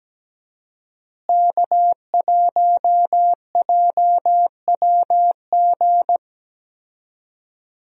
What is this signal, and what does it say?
Morse code, heard as a keyed tone, about 17 wpm, 700 Hz. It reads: K1JWG
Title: USA Call Signs - 5 Characters - Repeated 3x - 17wpm